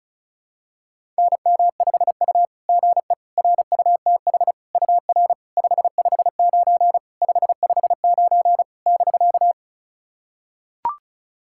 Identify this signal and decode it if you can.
Morse code, heard as a keyed tone, about 34 wpm, 700 Hz. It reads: NM5U GE RUTH UR 559 559 <BK>
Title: SOTA QSO Practice 35wpm